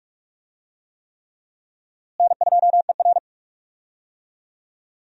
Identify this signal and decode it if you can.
Morse code, heard as a keyed tone, about 45 wpm, 700 Hz. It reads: N2ER
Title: USA Call Signs - 4 Characters - Repeated 3x - 45wpm